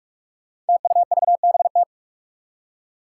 Morse code, heard as a keyed tone, about 46 wpm, 700 Hz. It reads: TUVBT
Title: Five Challenging Letter-Number Combinations 45wpm